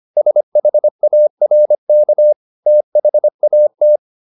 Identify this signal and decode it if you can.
Morse code, heard as a keyed tone, about 25 wpm, 600 Hz. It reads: SHARK THAT